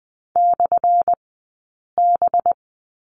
Morse code, heard as a keyed tone, about 20 wpm, 700 Hz. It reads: / B
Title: 190 Lesson 31d - Letter v - Two Characters 20wpm